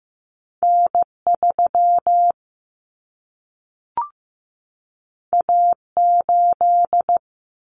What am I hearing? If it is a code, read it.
Morse code, heard as a keyed tone, about 15 wpm, 700 Hz. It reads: N 3 A 8
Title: Common Cut Numbers - 2-Digit 15wpm